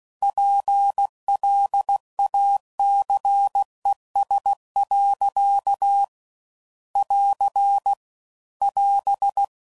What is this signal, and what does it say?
Morse code, heard as a keyed tone, about 16 wpm, 800 Hz. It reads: PLACES. <AR> <AS>